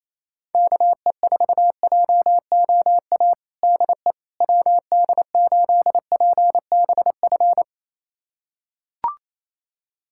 Morse code, heard as a keyed tone, about 28 wpm, 700 Hz. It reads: KE4JOA DE WD8PBF
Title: USA Callsign DE Callsign - 6 Characters - 28wpm